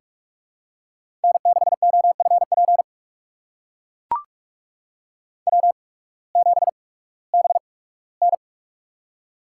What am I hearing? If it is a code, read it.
Morse code, heard as a keyed tone, about 44 wpm, spaced as for 21 wpm, 700 Hz. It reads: N6OFP W7BN
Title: USA Call Signs - 8x Character Spacing - 45wpm